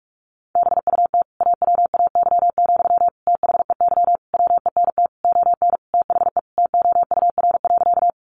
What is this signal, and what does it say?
Morse code, heard as a keyed tone, about 44 wpm, 700 Hz. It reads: BUT AWAY, THEY WENT ON THE TOUR.
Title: Sentences from Top 1400 Words 45wpm